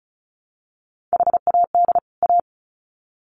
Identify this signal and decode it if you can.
Morse code, heard as a keyed tone, about 34 wpm, 700 Hz. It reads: HAD A
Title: Sets of 2 Words 35wpm